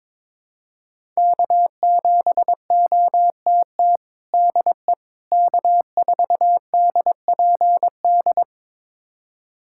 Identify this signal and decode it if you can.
Morse code, heard as a keyed tone, about 21 wpm, 700 Hz. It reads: K7OTT DE K4DPD
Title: USA Callsign DE Callsign - 5 Characters - 22wpm